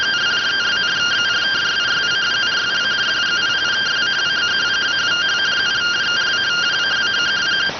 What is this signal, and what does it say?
Two-tone teleprinter signal, about 170 Hz shift.